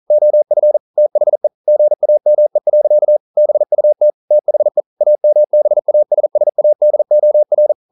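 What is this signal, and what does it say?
Morse code, heard as a keyed tone, about 42 wpm, 600 Hz. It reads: OF THE GAME. BUT THE AMBASSADOR
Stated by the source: Morse code practice at 41 wpm, from CW med Gnister.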